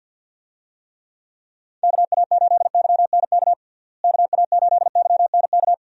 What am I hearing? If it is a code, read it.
Morse code, heard as a keyed tone, about 47 wpm, 700 Hz. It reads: KA8YNK KA8YNK
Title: USA Call Signs - 6 Characters - Repeated 3x - 50wpm